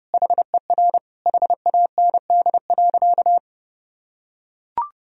Morse code, heard as a keyed tone, about 32 wpm, 700 Hz. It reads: HER HAND.
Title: Sentences from Top 1000 Words - Review Part 2 - 30wpm